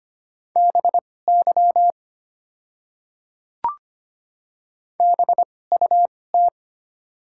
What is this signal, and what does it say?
Morse code, heard as a keyed tone, about 25 wpm, spaced as for 18 wpm, 690 Hz. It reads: BY BUT
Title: 2x Character Spacing - Top 100 Words - 25wpm